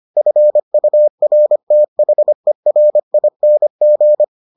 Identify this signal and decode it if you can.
Morse code, heard as a keyed tone, about 25 wpm, 600 Hz. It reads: FURTHERING